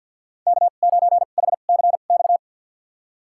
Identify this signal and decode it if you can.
Morse code, heard as a keyed tone, about 48 wpm, spaced as for 35 wpm, 700 Hz. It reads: K9HCX
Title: USA Call Signs - 5 Characters - 2x Character Spacing - 50wpm